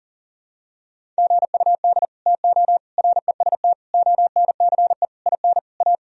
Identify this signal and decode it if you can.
Morse code, heard as a keyed tone, about 40 wpm, 700 Hz. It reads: GUD TO REST ONCE IN A
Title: Sentences from Top 500 Words and Abbreviations 40wpm